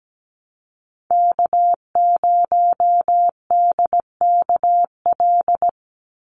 Morse code, heard as a keyed tone, about 17 wpm, 700 Hz. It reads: K0DKL
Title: USA Call Signs 17wpm